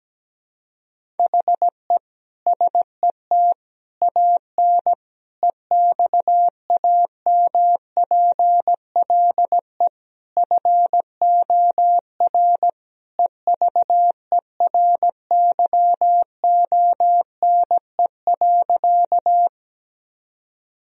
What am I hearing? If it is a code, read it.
Morse code, heard as a keyed tone, about 17 wpm, 700 Hz. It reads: HE SET AN EXAMPLE FOR EVERYONE.
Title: Sentences from Top 1000 Words - Review Part 1 - 17wpm